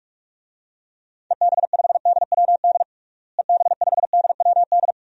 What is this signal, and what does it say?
Morse code, heard as a keyed tone, about 47 wpm, 700 Hz. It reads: EB5DWD EB5DWD
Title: DXCC-100 Call Signs - 6 Characters - Repeated 3x - 45wpm